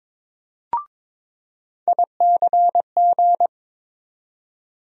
Morse code, heard as a keyed tone, about 22 wpm, 700 Hz. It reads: ICG
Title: USA Call Sign Postfixes - 3 Letters 22wpm